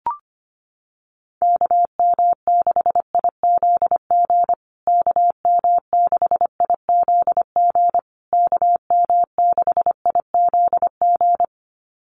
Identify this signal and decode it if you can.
Morse code, heard as a keyed tone, about 25 wpm, 700 Hz. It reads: KM6IZG KM6IZG KM6IZG
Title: USA Call Signs Repeated 3x 25wpm